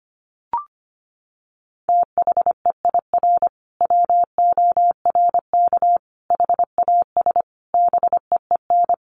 Morse code, heard as a keyed tone, about 25 wpm, 700 Hz. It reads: THEIR WORK HAS BEEN
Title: Sentences from Top 1500 Words 25wpm